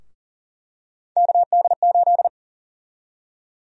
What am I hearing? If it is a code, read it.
Morse code, heard as a keyed tone, about 40 wpm, 700 Hz. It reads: KD8
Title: USA Call Sign Prefixes - 40wpm